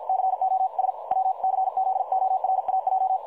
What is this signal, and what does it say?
Morse code, heard as a keyed tone, about 50 wpm, 760 Hz. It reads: COS WCZORAJ
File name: dlazaawansowanych / nagraniazpasma / websdr recording 3565.0kHz